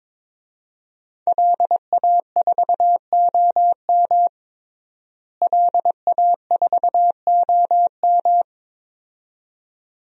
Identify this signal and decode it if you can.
Morse code, heard as a keyed tone, about 22 wpm, 700 Hz. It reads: LA4OM LA4OM
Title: DXCC-100 Call Signs - 5 Characters - Repeated 2x - 22wpm